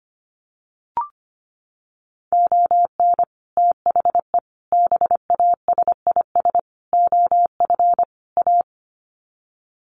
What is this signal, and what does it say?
Morse code, heard as a keyed tone, about 25 wpm, 700 Hz. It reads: ON THE BASIS OF A
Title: Sets of 5 Words Encore 25wpm